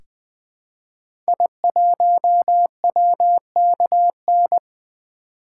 Morse code, heard as a keyed tone, about 20 wpm, 700 Hz. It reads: I1WKN
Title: 2023 SOTA Activator Call Signs - 20wpm